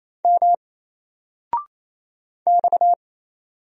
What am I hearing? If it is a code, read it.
Morse code, heard as a keyed tone, about 27 wpm, 700 Hz. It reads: M X
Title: Single Letter-Number 28wpm